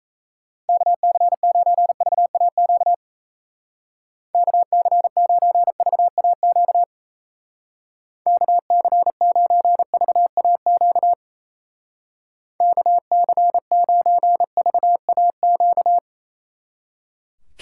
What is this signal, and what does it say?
Morse code, heard as a keyed tone, about 34 wpm, 700 Hz. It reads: KC9VAQ KC9VAQ KC9VAQ KC9VAQ
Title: Speed Racing - USA 2x3 Call Signs 28wpm